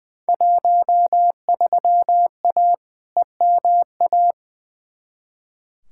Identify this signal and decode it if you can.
Morse code, heard as a keyed tone, about 20 wpm, 700 Hz. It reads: 13A EMA
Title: Field Day Exchanges Encore - 20wpm